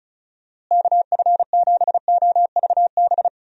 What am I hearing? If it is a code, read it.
Morse code, heard as a keyed tone, about 36 wpm, 700 Hz. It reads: KF7OVB
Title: USA Call Signs - 6 Characters - Repeated 2x - 35wpm